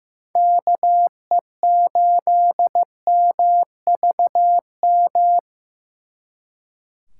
Morse code, heard as a keyed tone, about 18 wpm, 700 Hz. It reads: KE8MVM
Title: Speed Racing - USA 2x3 Call Signs 15wpm